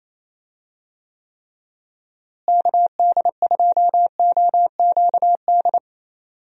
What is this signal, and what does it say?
Morse code, heard as a keyed tone, about 28 wpm, 700 Hz. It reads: KD2OQD
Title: USA Call Signs - 6 Characters - Repeated 3x - 28wpm